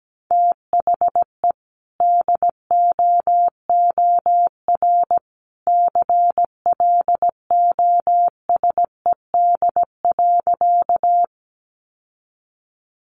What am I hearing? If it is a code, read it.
Morse code, heard as a keyed tone, about 17 wpm, 700 Hz. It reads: THE DOOR CLOSED.
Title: Sentences from Top 1400 Words 17wpm